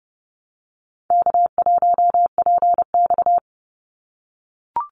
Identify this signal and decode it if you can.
Morse code, heard as a keyed tone, about 30 wpm, 700 Hz. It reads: K1PX
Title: Call Signs - 30wpm